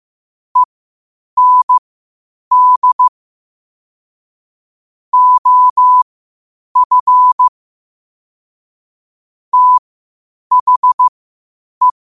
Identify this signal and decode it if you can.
Morse code, heard as a keyed tone, about 15 wpm, spaced as for 8 wpm, 1000 Hz. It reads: END OF THE
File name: qotdfiles / 8wpm